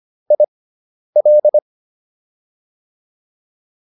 Morse code, heard as a keyed tone, about 25 wpm, spaced as for 10 wpm, 600 Hz. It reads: IL